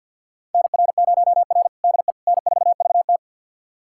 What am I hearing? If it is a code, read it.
Morse code, heard as a keyed tone, about 46 wpm, 700 Hz. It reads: NR0R DE N4VT
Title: USA Callsign DE Callsign - 4 Characters - 50wpm